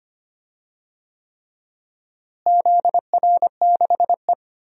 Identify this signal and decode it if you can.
Morse code, heard as a keyed tone, about 25 wpm, 700 Hz. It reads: ZR6E
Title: DXCC-100 Call Signs - 4 Characters - Repeated 3x - 25wpm